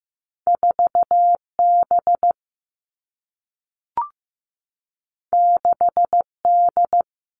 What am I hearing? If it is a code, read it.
Morse code, heard as a keyed tone, about 15 wpm, 700 Hz. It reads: 4B 6D
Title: All Cut Numbers - 2-Digit 15wpm